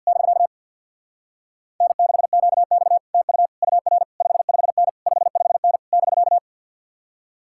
Morse code, heard as a keyed tone, about 61 wpm, 700 Hz. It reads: <BK> N6QX TU UR 55N 55N <BK>
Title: Speed-Racing Common QSO Phrases Encore - 50wpm